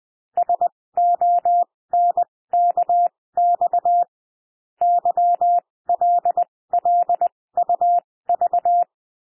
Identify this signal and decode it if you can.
Morse code, heard as a keyed tone, about 20 wpm, spaced as for 16 wpm, 700 Hz. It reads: SONKX YLLUV
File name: cwmp3 / L1 SOUND LETTERS 20 wpm